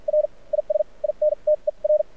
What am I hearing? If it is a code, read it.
Morse code, heard as a keyed tone, about 53 wpm, spaced as for 40 wpm, 600 Hz. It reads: R IS INTER
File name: archived / CRSnet